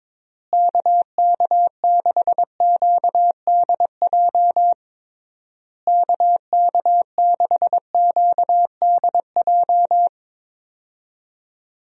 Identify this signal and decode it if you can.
Morse code, heard as a keyed tone, about 22 wpm, 700 Hz. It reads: KK6QDJ KK6QDJ
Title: USA Call Signs - 6 Characters - Repeated 3x - 22wpm